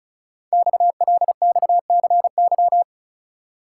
Morse code, heard as a keyed tone, about 36 wpm, 700 Hz. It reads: XLXCY
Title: Code Groups - 5 Letters - 35wpm